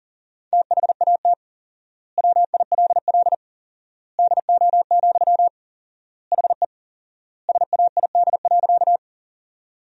Morse code, heard as a keyed tone, about 37 wpm, 700 Hz. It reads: THAT WILL DO, HE SAID.
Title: 4x Word-Spacing - Sentences from Top 100 Words - 40wpm